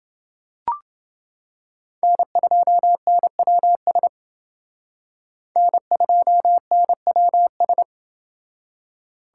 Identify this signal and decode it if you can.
Morse code, heard as a keyed tone, about 28 wpm, 700 Hz. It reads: N2NWS N2NWS